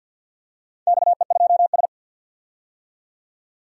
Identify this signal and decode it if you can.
Morse code, heard as a keyed tone, about 51 wpm, 700 Hz. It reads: XE2S
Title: DXCC-100 Call Signs - 4 Characters - 50wpm